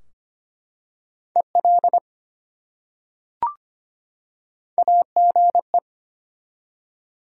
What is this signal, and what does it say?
Morse code, heard as a keyed tone, about 25 wpm, 700 Hz. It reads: EL AGE
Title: Top 180 QSO Elements 25wpm